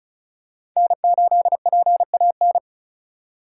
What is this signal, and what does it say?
Morse code, heard as a keyed tone, about 34 wpm, 700 Hz. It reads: N8PAN
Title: USA Call Signs - 5 Characters - 4x Character Spacing - 35wpm